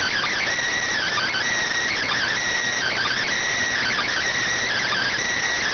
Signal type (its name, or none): Emergency Alert System (SAME) tones